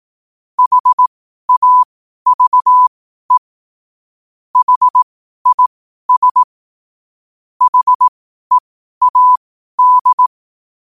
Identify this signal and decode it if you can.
Morse code, heard as a keyed tone, about 18 wpm, spaced as for 12 wpm, 1000 Hz. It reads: HAVE HIS HEAD
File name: qotdfiles / 12wpm